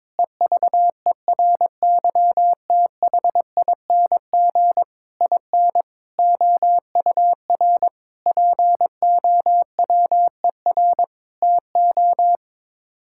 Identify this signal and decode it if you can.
Morse code, heard as a keyed tone, about 21 wpm, 700 Hz. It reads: EVERYTHING IN OUR POWER TO
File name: practice / 22wpm / Sets of 8 Words - 22wpm